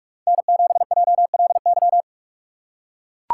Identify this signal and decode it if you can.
Morse code, heard as a keyed tone, about 43 wpm, 700 Hz. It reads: N7JLY
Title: USA Call Signs - 5 Characters - 2x Character Spacing - 45wpm